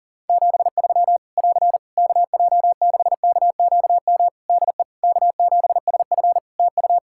Morse code, heard as a keyed tone, about 40 wpm, 700 Hz. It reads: 73 <AR> KJ6KQM DE K7SF TU
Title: Practice QSO - Exchange 4 - Ending - 40wpm